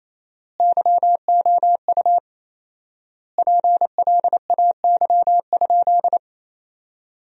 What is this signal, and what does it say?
Morse code, heard as a keyed tone, about 28 wpm, 700 Hz. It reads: YOU PLAY?
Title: 4x Word-Spacing - Sentences from Top 200 Words - 28wpm